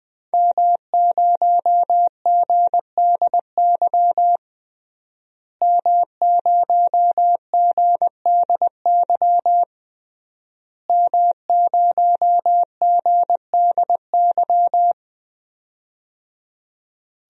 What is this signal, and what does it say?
Morse code, heard as a keyed tone, about 20 wpm, 700 Hz. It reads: M0GDY M0GDY M0GDY
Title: DXCC-100 Call Signs - 5 Characters - Repeated 3x - 20wpm